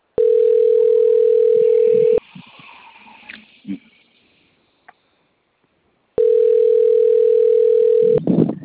Ringback tone (North America), 440 + 480 Hz.